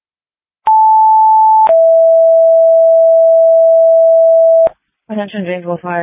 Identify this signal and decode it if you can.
Two-tone page, heard as a one-second tone, then a three-second tone, about 880 then 660 Hz, with speech. It is heard from 0.7 to 4.7 s.